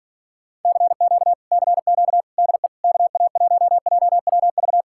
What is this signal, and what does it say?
Morse code, heard as a keyed tone, about 46 wpm, 700 Hz. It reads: CQ CQ DE KA1JWV